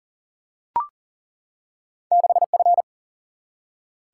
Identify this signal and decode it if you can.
Morse code, heard as a keyed tone, about 40 wpm, 700 Hz. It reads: 6F
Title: Single-Letter Single-Number 40wpm